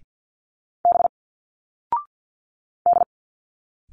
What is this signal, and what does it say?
Morse code, heard as a keyed tone, about 50 wpm, 700 Hz. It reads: B D
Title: Single Letters 50wpm